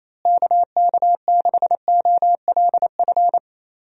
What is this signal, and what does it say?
Morse code, heard as a keyed tone, about 28 wpm, 700 Hz. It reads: KK6OLF
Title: USA Call Signs - 6 Characters - Repeated 3x - 28wpm